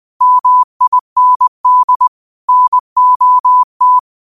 Morse code, heard as a keyed tone, about 20 wpm, 1000 Hz. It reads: MIND NOT
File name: qotdfiles / 20wpm